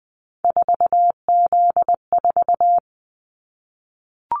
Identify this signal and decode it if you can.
Morse code, heard as a keyed tone, about 20 wpm, 700 Hz. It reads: 4Z4
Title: Callsign Prefixes for Top 50 DXCC - 20wpm